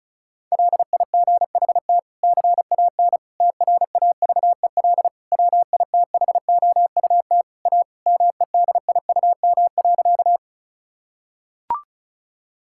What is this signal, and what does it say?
Morse code, heard as a keyed tone, about 34 wpm, 700 Hz. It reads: LIGHT CAN TRAVEL WITHOUT A MEDIUM.
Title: Sentences from Top 1000 Words - Review Part 6 - 35wpm